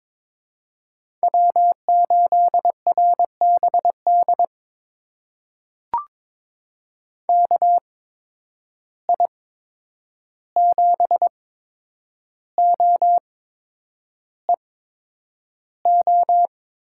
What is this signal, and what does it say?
Morse code, heard as a keyed tone, about 22 wpm, spaced as for 9 wpm, 700 Hz. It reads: W8RBD KI7OEO
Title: USA Call Signs - 8x Character Spacing - 22wpm